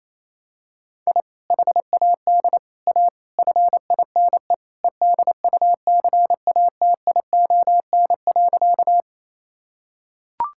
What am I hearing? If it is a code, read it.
Morse code, heard as a keyed tone, about 28 wpm, 700 Hz. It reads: I HAD A FINE EDUCATION.
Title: Sentences from Top 1000 Words - Review Part 6 - 28wpm